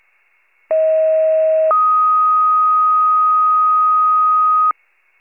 Two-tone page, heard as a one-second tone, then a three-second tone, about 640 then 1250 Hz. It is heard from 0.7 to 4.7 s.
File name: wcalert